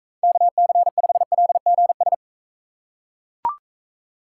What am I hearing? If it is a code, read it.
Morse code, heard as a keyed tone, about 44 wpm, 700 Hz. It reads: KC5LGS